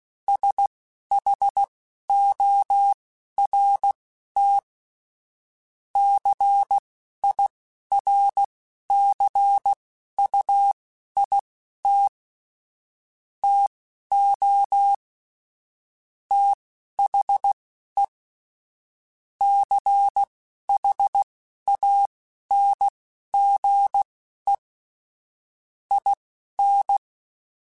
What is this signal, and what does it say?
Morse code, heard as a keyed tone, about 16 wpm, spaced as for 11 wpm, 800 Hz. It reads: SHORT CIRCUIT TO THE CHANGE IN
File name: Technical Subjects / Technical Subjects-mp3s / XC @10